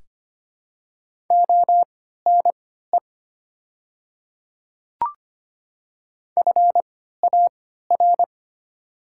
Morse code, heard as a keyed tone, about 25 wpm, spaced as for 14 wpm, 700 Hz. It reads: ONE FAR
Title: 3x Character Spacing - Top 300 Words - 25wpm